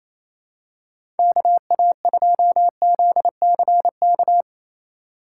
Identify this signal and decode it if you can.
Morse code, heard as a keyed tone, about 28 wpm, 700 Hz. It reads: KA2ZCK
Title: USA Call Signs - 3x Character Spacing - 28wpm